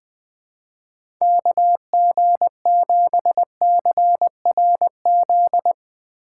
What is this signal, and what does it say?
Morse code, heard as a keyed tone, about 20 wpm, 700 Hz. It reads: KG7CRZ
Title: USA Call Signs - 6x Character Spacing - 20wpm